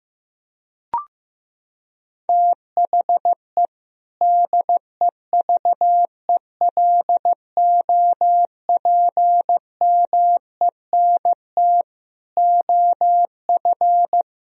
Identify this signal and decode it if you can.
Morse code, heard as a keyed tone, about 16 wpm, 700 Hz. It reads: THE DEVELOPMENT OF
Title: Sentences from Top 1000 Words - Review Part 4 - 15wpm